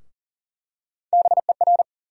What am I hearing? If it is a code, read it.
Morse code, heard as a keyed tone, about 40 wpm, 700 Hz. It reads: BER
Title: USA Call Sign Postfixes - 3 Letters 40wpm